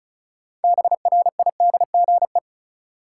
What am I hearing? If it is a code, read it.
Morse code, heard as a keyed tone, about 36 wpm, 700 Hz. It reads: BRIDGE
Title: Sentences from Top 1200 Words - New Word Drill - 35wpm